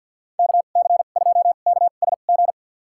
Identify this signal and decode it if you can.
Morse code, heard as a keyed tone, about 48 wpm, spaced as for 35 wpm, 700 Hz. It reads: KC2KSG
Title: USA Call Signs - 2x Character Spacing - 50wpm